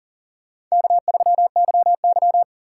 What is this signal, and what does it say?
Morse code, heard as a keyed tone, about 40 wpm, 700 Hz. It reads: K3YY
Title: USA Call Signs - 4 Characters - 3x Character Spacing - 40wpm